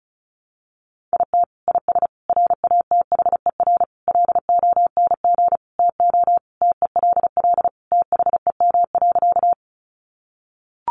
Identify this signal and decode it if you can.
Morse code, heard as a keyed tone, about 34 wpm, 700 Hz. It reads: IT IS RATHER LONG TO TELL THEM.
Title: Sentences from Top 700 Words 35wpm